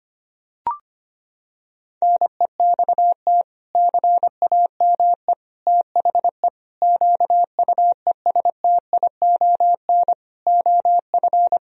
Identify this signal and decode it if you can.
Morse code, heard as a keyed tone, about 25 wpm, 700 Hz. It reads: NEXT CAME THE QUESTION OF
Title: Sentences from Top 1000 Words - Review Part 4 - 25wpm